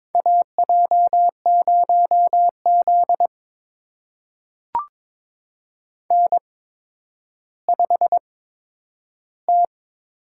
Morse code, heard as a keyed tone, about 22 wpm, spaced as for 11 wpm, 700 Hz. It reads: AJ0Z N5T
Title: USA Call Signs - 4 Characters - 8x Character Spacing - 22wpm